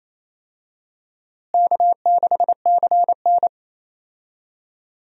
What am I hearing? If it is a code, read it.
Morse code, heard as a keyed tone, about 28 wpm, 700 Hz. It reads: K6CN
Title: USA Call Signs - 4 Characters - Repeated 3x - 28wpm